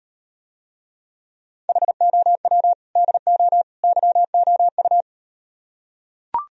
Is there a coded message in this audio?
Yes: Morse code from 1.7 to 5.0 s, heard as a keyed tone, about 38 wpm, 700 Hz. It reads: HOW DO YOU